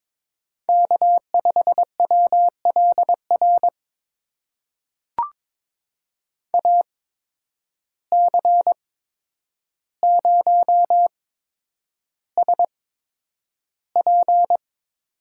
Morse code, heard as a keyed tone, about 22 wpm, spaced as for 9 wpm, 700 Hz. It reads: K5WLR AC0SP
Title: USA Call Signs - 5 Characters - 8x Character Spacing - 22wpm